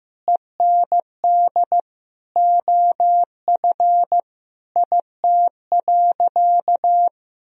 Morse code, heard as a keyed tone, about 16 wpm, 700 Hz. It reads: END OF IT.
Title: Sentences from Top 1000 Words - Review Part 1 - 15wpm